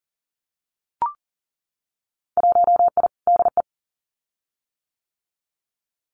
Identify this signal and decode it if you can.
Morse code, heard as a keyed tone, about 38 wpm, 700 Hz. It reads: 1I DE
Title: Winter Field Day Exchanges Encore 40wpm